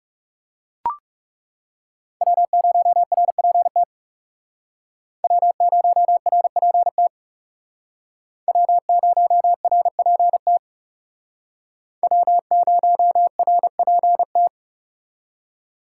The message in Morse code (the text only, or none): W0RPT W0RPT W0RPT W0RPT